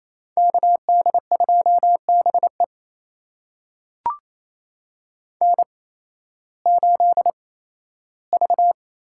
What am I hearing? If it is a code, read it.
Morse code, heard as a keyed tone, about 28 wpm, spaced as for 15 wpm, 700 Hz. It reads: KD2BE N8V
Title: USA Call Signs - 5 Characters - 8x Character Spacing - 28wpm